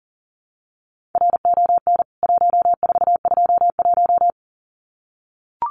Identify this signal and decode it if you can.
Morse code, heard as a keyed tone, about 40 wpm, 700 Hz. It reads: RON 1421
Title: CWT Exchange 40wpm